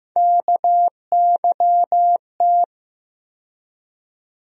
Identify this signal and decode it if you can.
Morse code, heard as a keyed tone, about 15 wpm, 700 Hz. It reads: KYT
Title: USA Call Sign Postfixes - 3 Letters 15wpm